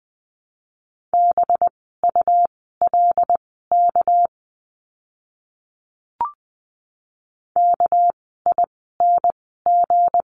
Morse code, heard as a keyed tone, about 20 wpm, spaced as for 15 wpm, 700 Hz. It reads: BULK KING
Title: 199 Lesson 32f - Letter k - Words 20wpm